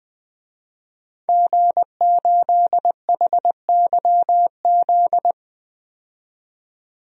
Morse code, heard as a keyed tone, about 21 wpm, 700 Hz. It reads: G8HYZ
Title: DXCC-100 Call Signs - 5 Characters - 20wpm